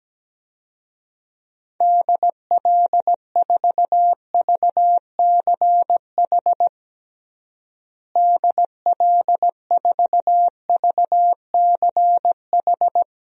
Morse code, heard as a keyed tone, about 17 wpm, 700 Hz. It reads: DL4VCH DL4VCH
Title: DXCC-100 Call Signs - 6 Characters - Repeated 2x - 17wpm